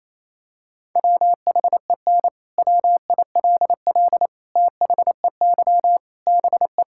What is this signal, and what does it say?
Morse code, heard as a keyed tone, about 28 wpm, 700 Hz. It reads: WHEN WILL THEY BE
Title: Sentences from Top 1000 Words - Review Part 6 - 28wpm